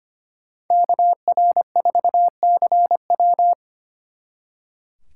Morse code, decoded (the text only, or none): KR4CW